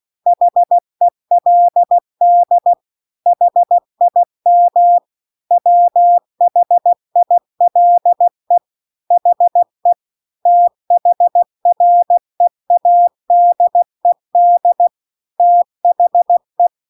Morse code, heard as a keyed tone, about 16 wpm, 700 Hz. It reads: HELD HIM WHILE HE THREADED THE
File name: CW-The-Big-Twoheaded-River-2-16WPM